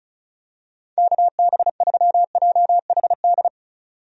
Morse code, heard as a keyed tone, about 36 wpm, 700 Hz. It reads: KB3JHD
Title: USA Call Signs - 6 Characters - 35wpm